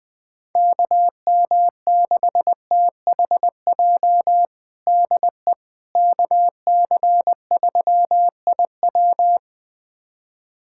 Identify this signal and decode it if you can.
Morse code, heard as a keyed tone, about 21 wpm, 700 Hz. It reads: KM6THJ DE KC3IW
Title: USA Callsign DE Callsign 20wpm